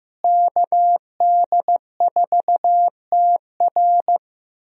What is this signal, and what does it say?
Morse code, heard as a keyed tone, about 15 wpm, 700 Hz. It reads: KD4TR
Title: USA Call Signs - 5 Characters - Repeated 3x - 15wpm